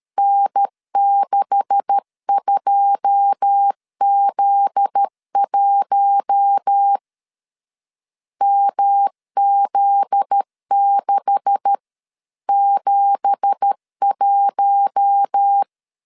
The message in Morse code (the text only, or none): N62Z1 MZ671